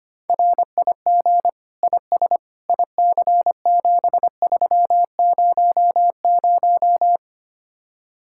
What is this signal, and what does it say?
Morse code, heard as a keyed tone, about 31 wpm, 700 Hz. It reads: RIG IS IC7300
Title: Speed-Racing Common QSO Phrases Encore - 25wpm